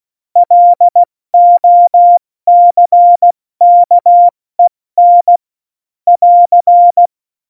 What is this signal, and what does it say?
Morse code, heard as a keyed tone, about 16 wpm, 700 Hz. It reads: LOCKEN <AR>
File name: Morsekurs / Witze / Witz 16. Januar 2018, 70 F80